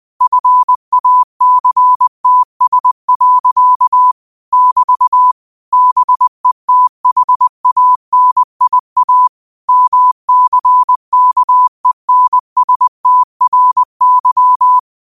Morse code, heard as a keyed tone, about 20 wpm, 1000 Hz. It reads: FACTS. <BT> BETHANIA MCKENSTRY
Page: New quotes every day in morse code at 20 Words per minute.